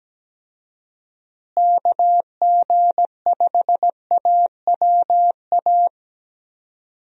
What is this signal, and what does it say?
Morse code, heard as a keyed tone, about 17 wpm, 700 Hz. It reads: KG5AWA
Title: USA Call Signs - 6 Characters - Repeated 2x - 17wpm